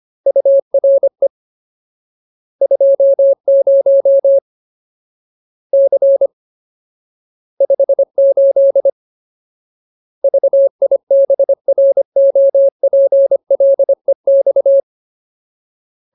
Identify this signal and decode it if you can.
Morse code, heard as a keyed tone, about 25 wpm, 550 Hz. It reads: URE 20 C 58 VIBROPLEX
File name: Ultimate Word List / 25WPM Ultimate